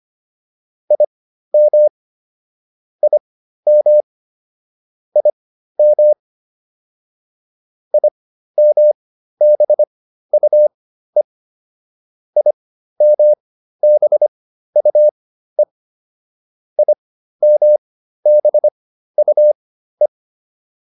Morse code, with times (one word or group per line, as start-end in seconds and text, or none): 0.9-1.9 IM
3.0-4.0 IM
5.2-6.1 IM
7.9-11.2 IMBUE
12.4-15.6 IMBUE
16.8-20.1 IMBUE